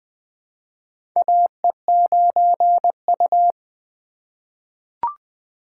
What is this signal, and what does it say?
Morse code, heard as a keyed tone, about 20 wpm, 700 Hz. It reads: AE9U
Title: USA Call Signs - 4 Characters - 2x Character Spacing - 20wpm